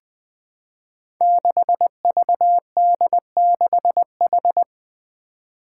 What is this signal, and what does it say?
Morse code, heard as a keyed tone, about 20 wpm, 700 Hz. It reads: 6VD6H
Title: Five Challenging Letter-Number Combinations 20wpm